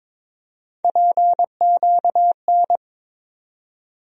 Morse code, heard as a keyed tone, about 22 wpm, 700 Hz. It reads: PQN
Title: USA Call Sign Postfixes - 3 Letters 22wpm